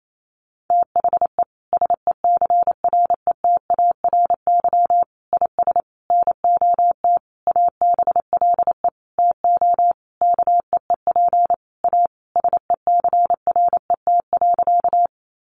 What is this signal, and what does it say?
Morse code, heard as a keyed tone, about 28 wpm, 700 Hz. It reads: THE SECRETARY IS NOT ABLE TO KEEP A SECRET.
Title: Sentences from Top 1300 Words 28wpm